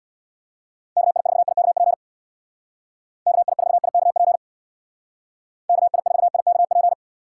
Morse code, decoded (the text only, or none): KI4IKP KI4IKP KI4IKP